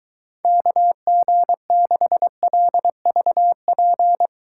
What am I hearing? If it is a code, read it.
Morse code, heard as a keyed tone, about 23 wpm, 700 Hz. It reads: KG6LVP